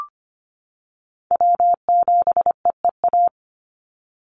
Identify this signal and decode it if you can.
Morse code, heard as a keyed tone, about 25 wpm, 700 Hz. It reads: W7EEA
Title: USA Call Signs 25wpm